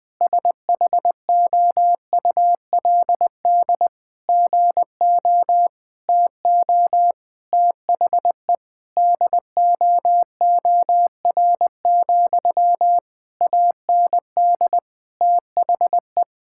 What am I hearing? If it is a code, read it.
Morse code, heard as a keyed tone, about 20 wpm, 700 Hz. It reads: SHOULD GO TO THE DOOR, AND THE
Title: Sentences from Top 1600 Words 20wpm